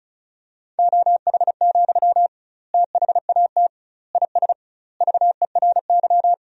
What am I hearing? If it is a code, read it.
Morse code, heard as a keyed tone, about 36 wpm, 700 Hz. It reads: OH, THAT IS VERY
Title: 2x Word-Spacing - Sentences from Top 500 Words - 35wpm